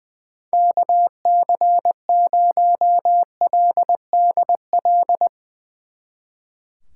Morse code, heard as a keyed tone, about 20 wpm, 700 Hz. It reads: KC0LDL
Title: Speed Racing - USA Call Signs 20wpm